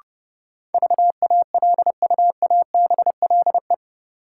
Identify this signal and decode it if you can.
Morse code, heard as a keyed tone, about 30 wpm, 700 Hz. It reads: VALUABLE
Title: Sentences from Top 1100 Words - New Word Drill Alt - 30wpm